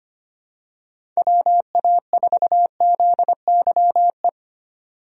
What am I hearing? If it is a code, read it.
Morse code, heard as a keyed tone, about 25 wpm, 700 Hz. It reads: WA4ZYE
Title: USA Call Signs - 6 Characters - 25wpm